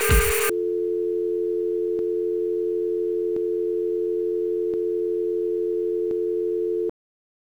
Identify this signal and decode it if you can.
Dial tone (North America), 350 + 440 Hz.